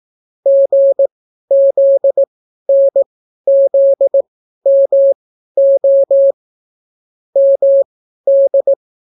Morse code, heard as a keyed tone, about 18 wpm, spaced as for 12 wpm, 550 Hz. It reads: GZNZMO MD